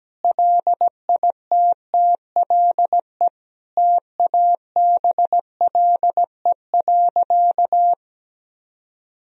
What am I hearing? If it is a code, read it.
Morse code, heard as a keyed tone, about 17 wpm, 700 Hz. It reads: LITTLE TABLE.
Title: Sentences from Top 1000 Words - Review Part 2 - 17wpm